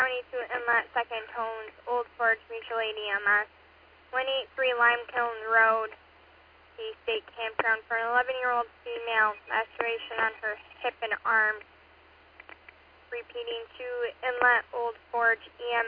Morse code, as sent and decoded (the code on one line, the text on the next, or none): none